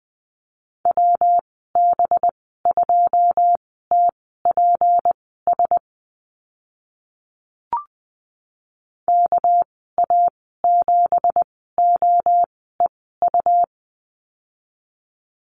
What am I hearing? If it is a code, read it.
Morse code, heard as a keyed tone, about 20 wpm, spaced as for 14 wpm, 700 Hz. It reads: WB2TPS KA7OEU
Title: 238 Lesson 38g - Letter q - Call Signs 20wpm